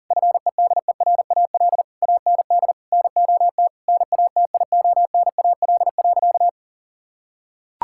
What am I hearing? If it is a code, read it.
Morse code, heard as a keyed tone, about 40 wpm, 700 Hz. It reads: FEDERAL AND NOT NATIONAL.
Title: Sentences from Top 1000 Words - Review Part 5 - 40wpm